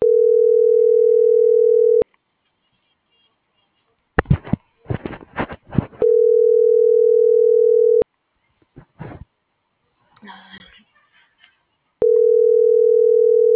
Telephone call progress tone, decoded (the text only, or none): ringback tone